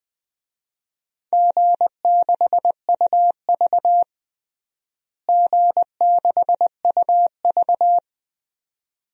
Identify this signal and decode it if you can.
Morse code, heard as a keyed tone, about 21 wpm, 700 Hz. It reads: G6UV G6UV
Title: DXCC-100 Call Signs - 4 Characters - Repeated 2x - 20wpm